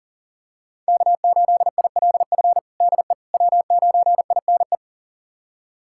Morse code, heard as a keyed tone, about 37 wpm, 700 Hz. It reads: K8ILF DE W9INE
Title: USA Callsign DE Callsign - 5 Characters - 40wpm